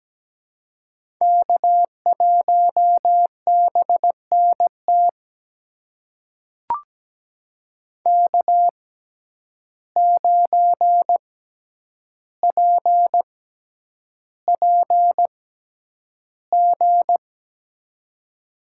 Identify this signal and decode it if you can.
Morse code, heard as a keyed tone, about 17 wpm, spaced as for 9 wpm, 700 Hz. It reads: K1BNT K9PPG
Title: USA Call Signs - 5 Characters - 6x Character Spacing - 17wpm